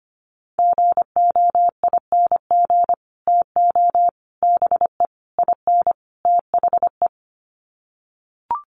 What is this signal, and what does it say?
Morse code, heard as a keyed tone, about 25 wpm, 700 Hz. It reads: GOING TO BE IN THE
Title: Sets of 5 Words Encore 25wpm